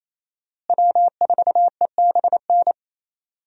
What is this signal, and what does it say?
Morse code, heard as a keyed tone, about 28 wpm, 700 Hz. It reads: W4EBN
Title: USA Call Signs - 5 Characters - Repeated 2x - 28wpm